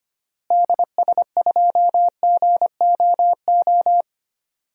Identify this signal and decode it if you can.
Morse code, heard as a keyed tone, about 25 wpm, 700 Hz. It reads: DS2GOO
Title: DXCC-100 Call Signs - 6 Characters - Repeated 3x - 25wpm